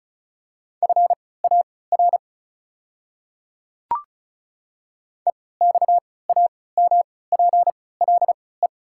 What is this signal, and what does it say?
Morse code, heard as a keyed tone, about 34 wpm, spaced as for 20 wpm, 700 Hz. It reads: FAR EXAMPLE
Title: 3x Character Spacing - Top 500 Words - 35wpm